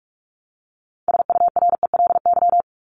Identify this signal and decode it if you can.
Morse code, heard as a keyed tone, about 44 wpm, 700 Hz. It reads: SURELY
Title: LY Suffix - Top 100 Words with Base Words 45wpm